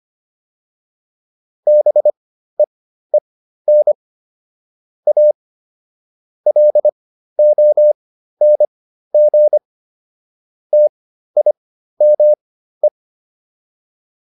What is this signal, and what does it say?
Morse code, heard as a keyed tone, about 25 wpm, spaced as for 13 wpm, 600 Hz. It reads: BEEN A LONG TIME